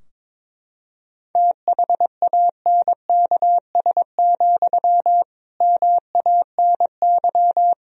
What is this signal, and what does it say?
Morse code, heard as a keyed tone, about 22 wpm, 700 Hz. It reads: THANKS, MANY
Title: Sentences from Top 1000 Words - Review Part 3 - 22wpm